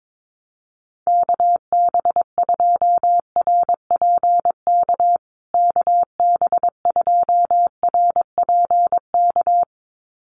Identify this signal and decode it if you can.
Morse code, heard as a keyed tone, about 22 wpm, 700 Hz. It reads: KB2RPK KB2RPK
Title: USA Call Signs Repeated 2x 22wpm